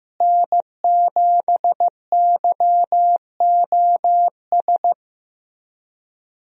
Morse code, heard as a keyed tone, about 15 wpm, 700 Hz. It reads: N7YOS
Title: USA Call Signs - 5 Characters - Repeated 2x - 15wpm